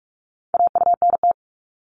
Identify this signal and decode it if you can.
Morse code, heard as a keyed tone, about 46 wpm, 700 Hz. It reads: AUNT
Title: Multi-letter Phonemes Repeated 3x 45wpm